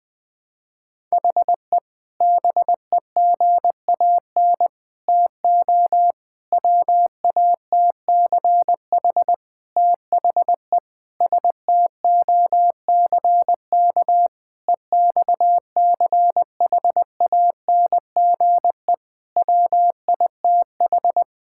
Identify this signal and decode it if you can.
Morse code, heard as a keyed tone, about 20 wpm, 700 Hz. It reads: HE BEGAN TO WATCH THE STOCK EXCHANGE WITH
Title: Sentences from Top 1000 Words - Review Part 6 - 20wpm